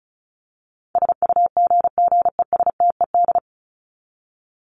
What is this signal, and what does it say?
Morse code, heard as a keyed tone, about 34 wpm, 700 Hz. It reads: SUGGESTED
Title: ED Suffix - Top 100 Words with Base Words 35wpm